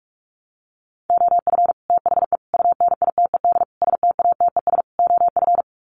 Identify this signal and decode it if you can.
Morse code, heard as a keyed tone, about 44 wpm, 700 Hz. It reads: OF THE UNITED STATES OF
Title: Sets of 5 Words Encore 45wpm